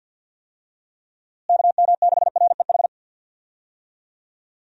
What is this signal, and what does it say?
Morse code, heard as a keyed tone, about 47 wpm, 700 Hz. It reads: KM6REH
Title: USA Call Signs - 6 Characters - Repeated 2x - 50wpm